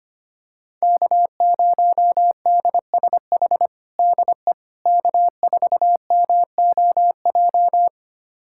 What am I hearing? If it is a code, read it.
Morse code, heard as a keyed tone, about 25 wpm, 700 Hz. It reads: K0DSH DE K4MOJ
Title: USA Callsign DE Callsign - 5 Characters - 25wpm